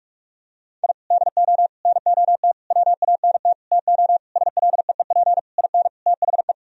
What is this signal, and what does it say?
Morse code, heard as a keyed tone, about 46 wpm, 700 Hz. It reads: I DO NOT WANT TO SLEEP IN THE
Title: Sentences from Top 1000 Words - Review Part 2 - 45wpm